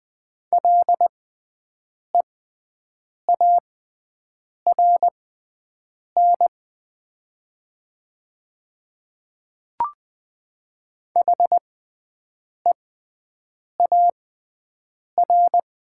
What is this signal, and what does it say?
Morse code, heard as a keyed tone, about 20 wpm, spaced as for 7 wpm, 700 Hz. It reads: LEARN HEAR
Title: 6x Character Spacing - Top 300 Words - 20wpm